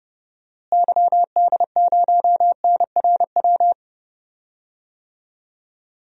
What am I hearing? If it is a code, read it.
Morse code, heard as a keyed tone, about 30 wpm, 700 Hz. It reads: YD0NRW
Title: DXCC-100 Call Signs - 6 Characters - Repeated 3x - 30wpm